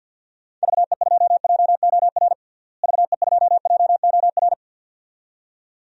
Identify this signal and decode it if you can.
Morse code, heard as a keyed tone, about 53 wpm, 700 Hz. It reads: VE2JOR VE2JOR
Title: DXCC-100 Call Signs - 6 Characters - Repeated 2x - 50wpm